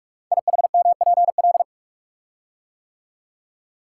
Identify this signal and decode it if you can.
Morse code, heard as a keyed tone, about 44 wpm, 700 Hz. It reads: IHMPL
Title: Code Groups - 5 Letters - 45wpm